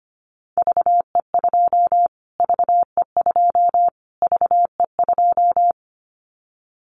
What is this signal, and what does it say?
Morse code, heard as a keyed tone, about 25 wpm, 700 Hz. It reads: VE2 VE2 VE2
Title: Callsign Prefixes for Top 25 DXCC - Repeated 3x - 25wpm